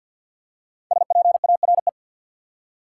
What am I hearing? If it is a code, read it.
Morse code, heard as a keyed tone, about 48 wpm, 700 Hz. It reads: SPARE
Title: Five Letter Words - 50wpm